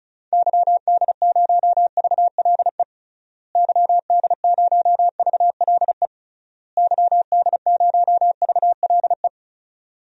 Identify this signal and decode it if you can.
Morse code, heard as a keyed tone, about 36 wpm, 700 Hz. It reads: YD0VLE YD0VLE YD0VLE
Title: DXCC-100 Call Signs - 6 Characters - Repeated 3x - 35wpm